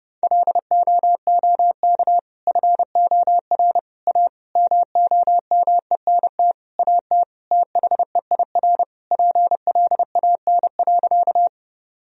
Morse code, heard as a keyed tone, about 30 wpm, 700 Hz. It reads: LOOK FOR A MOMENT AT THEIR PLAN.
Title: Sentences from Top 1000 Words - Review Part 2 - 30wpm